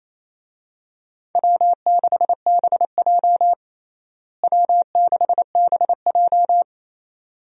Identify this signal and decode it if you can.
Morse code, heard as a keyed tone, about 28 wpm, 700 Hz. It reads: W6BJ W6BJ
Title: USA Call Signs - 4 Characters - Repeated 2x - 28wpm